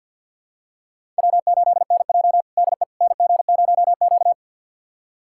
Morse code, heard as a keyed tone, about 49 wpm, 700 Hz. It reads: W8NJ DE NG0Q
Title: USA Callsign DE Callsign - 4 Characters - 50wpm